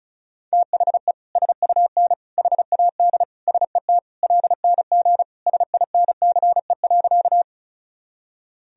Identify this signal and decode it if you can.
Morse code, heard as a keyed tone, about 34 wpm, 700 Hz. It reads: THE SUN HAD SET LNG SINCE.
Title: Sentences from Top 300 Words and Abbreviations 35wpm